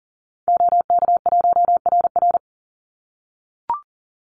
Morse code, heard as a keyed tone, about 40 wpm, 700 Hz. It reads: OK1RR
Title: Call Signs - 40wpm